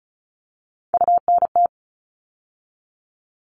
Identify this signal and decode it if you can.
Morse code, heard as a keyed tone, about 34 wpm, 700 Hz. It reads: UNT
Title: All Cut Numbers - 3-Digit 35wpm